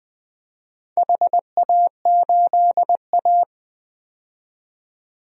Morse code, heard as a keyed tone, about 20 wpm, 700 Hz. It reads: HA8A
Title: DXCC-100 Call Signs - 4 Characters - 20wpm